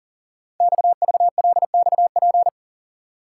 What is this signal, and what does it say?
Morse code, heard as a keyed tone, about 42 wpm, 700 Hz. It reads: XVLXP
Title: Code Groups - 5 Letters - 40wpm